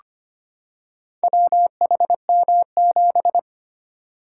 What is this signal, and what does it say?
Morse code, heard as a keyed tone, about 25 wpm, 700 Hz. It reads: WHM7
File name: practice / 25wpm / Four Letter-Number Combinations - 25wpm